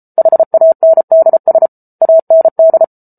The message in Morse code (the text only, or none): HANDS AND